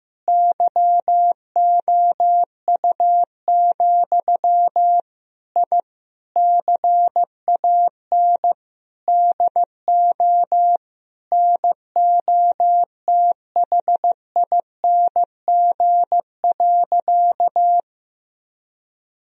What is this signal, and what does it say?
Morse code, heard as a keyed tone, about 15 wpm, 700 Hz. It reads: YOU, I CAN DO NOTHING.
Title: Sentences from Top 1000 Words - Review Part 3 - 15wpm